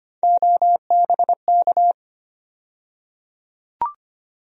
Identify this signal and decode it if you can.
Morse code, heard as a keyed tone, about 25 wpm, 700 Hz. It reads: OBK
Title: USA Call Sign Postfixes - 3 Letters 25wpm